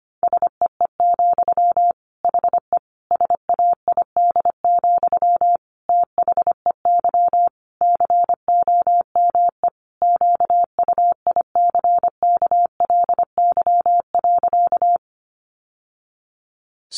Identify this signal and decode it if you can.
Morse code, heard as a keyed tone, about 25 wpm, 700 Hz. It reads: SEE, HE SAID, THEY COME QUICKLY.
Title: Sentences from Top 700 Words 25wpm